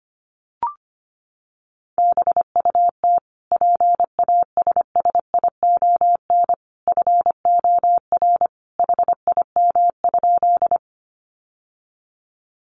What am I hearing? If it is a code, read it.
Morse code, heard as a keyed tone, about 25 wpm, 700 Hz. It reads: BUT PASSION FOR HIM?
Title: Sentences from Top 1300 Words 25wpm